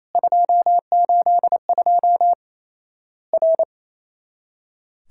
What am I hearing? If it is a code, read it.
Morse code, heard as a keyed tone, about 28 wpm, 700 Hz. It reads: 282 R
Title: Worked All Europe DX Contest - QTC Exchange - 28wpm